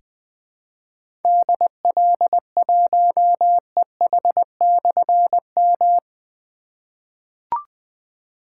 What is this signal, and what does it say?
Morse code, heard as a keyed tone, about 21 wpm, 700 Hz. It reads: DL1EH/M
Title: Difficult Call Signs 20wpm